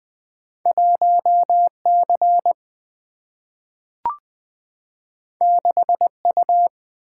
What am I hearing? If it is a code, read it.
Morse code, heard as a keyed tone, about 20 wpm, 700 Hz. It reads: 1C 6U
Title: Single-Letter Single-Number 20wpm